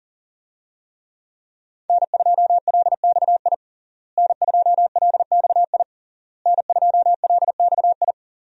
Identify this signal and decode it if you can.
Morse code, heard as a keyed tone, about 40 wpm, 700 Hz. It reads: N2LXI N2LXI N2LXI
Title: USA Call Signs - 5 Characters - Repeated 3x - 40wpm